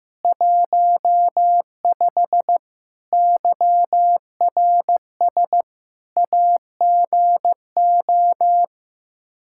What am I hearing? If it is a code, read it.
Morse code, heard as a keyed tone, about 14 wpm, 700 Hz. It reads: 15 YRS AGO
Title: Practice QSO - Exchange 3 - More Info - 15wpm